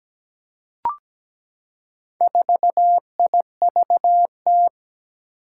Morse code, heard as a keyed tone, about 17 wpm, 700 Hz. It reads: 4IVT
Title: Four Challenging Letter-Number Combinations 17wpm